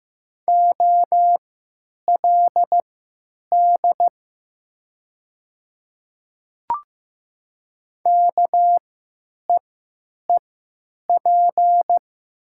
Morse code, heard as a keyed tone, about 15 wpm, spaced as for 9 wpm, 700 Hz. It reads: OLD KEEP
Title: 3x Character Spacing - Top 500 Words - 15wpm